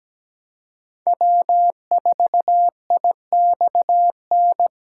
Morse code, heard as a keyed tone, about 20 wpm, 700 Hz. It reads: W4IXN
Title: Speed Racing - USA 1x3 Call Signs 17wpm